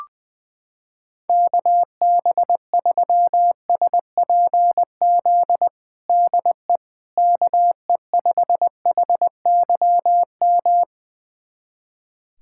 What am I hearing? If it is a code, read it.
Morse code, heard as a keyed tone, about 21 wpm, 700 Hz. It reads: KB3SPZ DE KE5HYM
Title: USA Callsign DE Callsign - 6 Characters - 20wpm